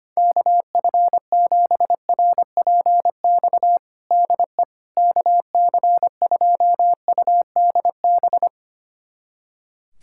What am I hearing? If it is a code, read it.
Morse code, heard as a keyed tone, about 25 wpm, 700 Hz. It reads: KF7RPX DE KC2UDB
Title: USA Callsign DE Callsign - 6 Characters - 25wpm